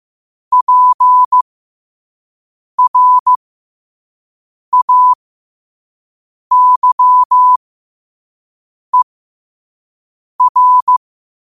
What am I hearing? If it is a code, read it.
Morse code, heard as a keyed tone, about 15 wpm, spaced as for 5 wpm, 1000 Hz. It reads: PRAYER
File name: qotdfiles / 5wpm